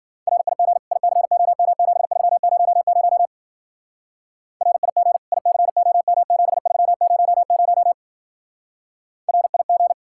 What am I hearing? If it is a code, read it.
Morse code, heard as a keyed tone, about 50 wpm, 700 Hz. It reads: RIG ICOM7300 RIG ICOM7300 RIG